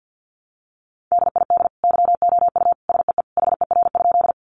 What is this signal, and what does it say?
Morse code, heard as a keyed tone, about 50 wpm, 700 Hz. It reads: DID YOU SEE HER?
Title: Sentences from Top 100 Words 50wpm